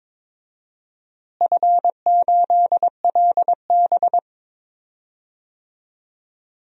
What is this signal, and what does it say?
Morse code, heard as a keyed tone, about 22 wpm, 700 Hz. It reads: F8LB
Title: DXCC-100 Call Signs - 4 Characters - Repeated 3x - 22wpm